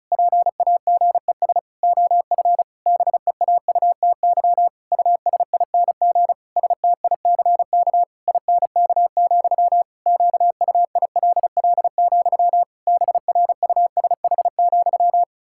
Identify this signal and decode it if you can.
Morse code, heard as a keyed tone, about 34 wpm, 700 Hz. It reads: PAGES OF BEAUTY USING STICK INK, QUILL, BRUSH,
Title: Pangram Challenge v2 - 35wpm